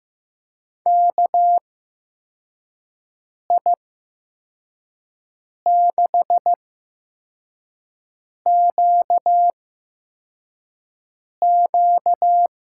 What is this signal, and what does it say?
Morse code, heard as a keyed tone, about 15 wpm, spaced as for 4 wpm, 700 Hz. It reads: KI6QQ
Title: USA Call Signs - 8x Character Spacing - 15wpm